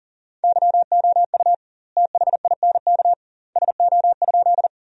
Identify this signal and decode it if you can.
Morse code, heard as a keyed tone, about 37 wpm, 700 Hz. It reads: YOU THINK SO?
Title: 2x Word-Spacing - Sentences from Top 200 Words - 40wpm